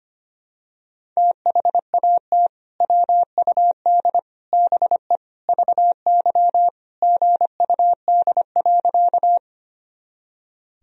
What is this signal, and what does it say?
Morse code, heard as a keyed tone, about 25 wpm, 700 Hz. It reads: THAT WUD BE VY GUD.
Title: Sentences from Top 200 Words and Abbreviations 25wpm